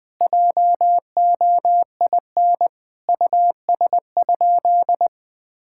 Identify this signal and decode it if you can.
Morse code, heard as a keyed tone, about 20 wpm, 700 Hz. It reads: JOIN US?
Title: Sentences with 4-Word Phrases - 20wpm